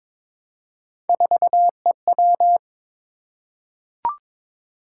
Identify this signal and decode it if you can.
Morse code, heard as a keyed tone, about 22 wpm, 700 Hz. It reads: 4EW
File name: practice / 22wpm / Three Letter-Number Combinations - 22wpm